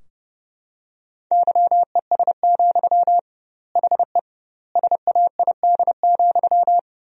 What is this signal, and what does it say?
Morse code, heard as a keyed tone, about 30 wpm, 700 Hz. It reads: YES, HE SAID,
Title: 2x Word-Spacing - Sentences from Top 500 Words - 30wpm